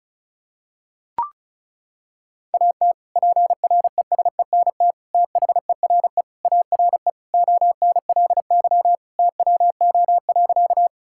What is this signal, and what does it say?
Morse code, heard as a keyed tone, about 34 wpm, 700 Hz. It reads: AT PRESENT THERE ARE ONLY TWO.
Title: Sentences from Top 1000 Words - Review Part 5 - 35wpm